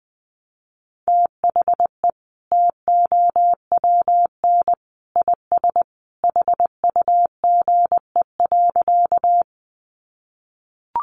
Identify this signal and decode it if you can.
Morse code, heard as a keyed tone, about 20 wpm, 700 Hz. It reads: THE TOWN IS HUGE.
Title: Sentences from Top 1100 Words 20wpm